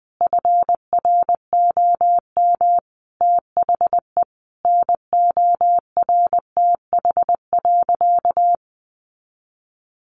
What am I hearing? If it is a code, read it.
Morse code, heard as a keyed tone, about 20 wpm, 700 Hz. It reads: FROM THE NORTH.
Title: Sentences from Top 1200 Words 20wpm